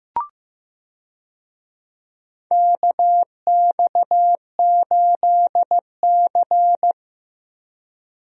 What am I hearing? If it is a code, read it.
Morse code, heard as a keyed tone, about 15 wpm, 700 Hz. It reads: KX8C
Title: USA Call Signs - 4 Characters - Repeated 2x - 15wpm